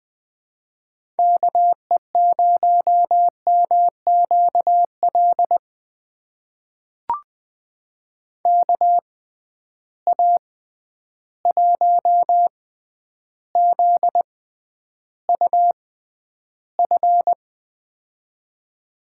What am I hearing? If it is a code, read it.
Morse code, heard as a keyed tone, about 20 wpm, spaced as for 10 wpm, 700 Hz. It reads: KE0MQL KA1ZUF
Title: USA Call Signs - 6 Characters - 6x Character Spacing - 20wpm